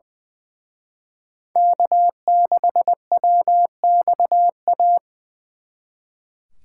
Morse code, heard as a keyed tone, about 24 wpm, 700 Hz. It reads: K6WXA
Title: Speed Racing - USA 1x3 Call Signs 20wpm